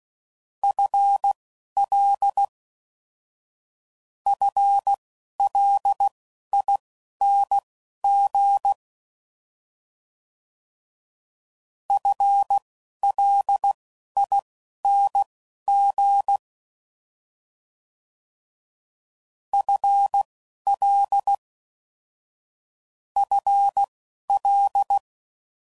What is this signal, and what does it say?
Morse code, heard as a keyed tone, about 16 wpm, spaced as for 11 wpm, 800 Hz. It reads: FL FLING FLING FL FL
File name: Prefixes / FLs & THs / FL Prefixes 3